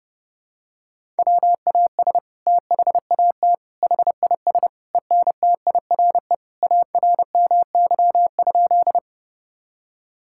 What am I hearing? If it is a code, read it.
Morse code, heard as a keyed tone, about 32 wpm, 700 Hz. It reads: WAS THAT HIS ENTIRE ARMY?
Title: Sentences from Top 1000 Words - Review Part 5 - 30wpm